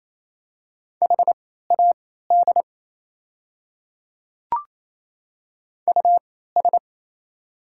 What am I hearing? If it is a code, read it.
Morse code, heard as a keyed tone, about 28 wpm, spaced as for 16 wpm, 700 Hz. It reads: HAD US
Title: 3x Character Spacing - Top 200 Words - 28wpm